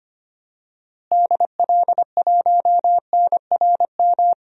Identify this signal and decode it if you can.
Morse code, heard as a keyed tone, about 25 wpm, 700 Hz. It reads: DL1NRM
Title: DXCC-100 Call Signs - 6 Characters - Repeated 2x - 25wpm